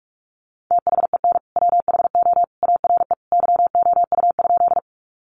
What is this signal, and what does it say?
Morse code, heard as a keyed tone, about 44 wpm, 700 Hz. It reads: THEN WHO ARE YOU?
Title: Sentences from Top 100 Words 45wpm